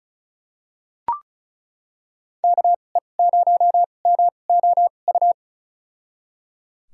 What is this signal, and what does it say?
Morse code, heard as a keyed tone, about 36 wpm, spaced as for 25 wpm, 700 Hz. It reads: KE0MOU
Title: USA Call Signs - 2x Character Spacing - 35wpm